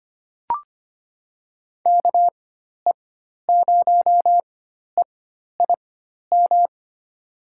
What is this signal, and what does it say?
Morse code, heard as a keyed tone, about 25 wpm, spaced as for 12 wpm, 700 Hz. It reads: KE0EIM
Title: USA Call Signs - 6 Characters - 4x Character Spacing - 25wpm